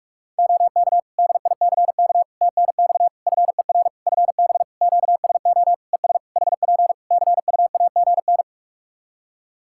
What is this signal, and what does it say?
Morse code, heard as a keyed tone, about 46 wpm, 700 Hz. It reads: OK DICK TNX FER FB QSO ES HP CUAGN
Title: Practice QSO - Exchange 4 - Ending - 45wpm